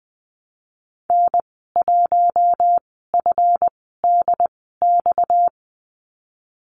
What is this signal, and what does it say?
Morse code, heard as a keyed tone, about 20 wpm, spaced as for 15 wpm, 700 Hz. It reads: N1FDX
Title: 238 Lesson 38g - Letter q - Call Signs 20wpm